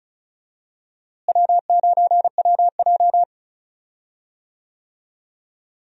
Morse code, heard as a keyed tone, about 36 wpm, 700 Hz. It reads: W9WJ
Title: USA Call Signs - 4 Characters - Repeated 3x - 35wpm